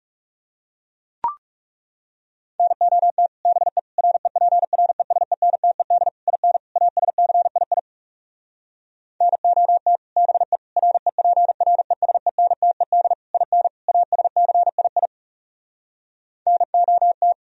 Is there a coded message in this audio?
Yes: Morse code from 2.6 to 17.3 s, heard as a keyed tone, about 40 wpm, 700 Hz. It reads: NOT BE REPRESENTED IN ASCII NOT BE REPRESENTED IN ASCII NOT